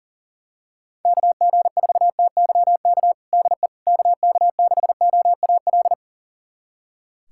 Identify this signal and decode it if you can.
Morse code, heard as a keyed tone, about 40 wpm, 700 Hz. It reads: KG4TYK DE KK6OAL
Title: USA Callsign DE Callsign - 6 Characters - 40wpm